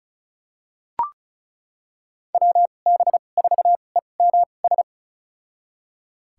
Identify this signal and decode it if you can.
Morse code, heard as a keyed tone, about 34 wpm, spaced as for 25 wpm, 700 Hz. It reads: WB4EMS
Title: USA Call Signs - 2x Character Spacing - 35wpm